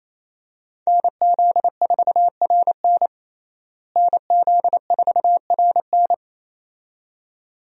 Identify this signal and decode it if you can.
Morse code, heard as a keyed tone, about 27 wpm, 700 Hz. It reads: NZ4RN NZ4RN
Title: USA Call Signs - 5 Characters - Repeated 2x - 28wpm